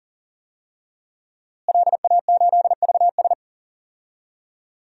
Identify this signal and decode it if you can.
Morse code, heard as a keyed tone, about 40 wpm, 700 Hz. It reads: LA8VS
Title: DXCC-100 Call Signs - 5 Characters - Repeated 3x - 40wpm